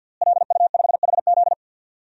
Morse code, heard as a keyed tone, about 48 wpm, 700 Hz. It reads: LU5HZ
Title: DXCC-100 Call Signs - 5 Characters - Repeated 2x - 50wpm